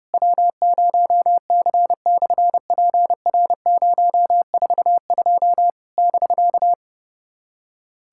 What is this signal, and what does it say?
Morse code, heard as a keyed tone, about 30 wpm, 700 Hz. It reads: W0C/PR042 <BK>
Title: SOTA QSO Practice 30wpm